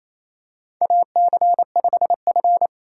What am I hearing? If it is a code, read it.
Morse code, heard as a keyed tone, about 28 wpm, 700 Hz. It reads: AC5F
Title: USA Call Signs - 4 Characters - 6x Character Spacing - 28wpm